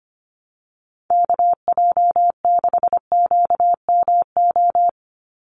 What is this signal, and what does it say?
Morse code, heard as a keyed tone, about 25 wpm, 700 Hz. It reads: KJ6QMO
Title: USA Call Signs 25wpm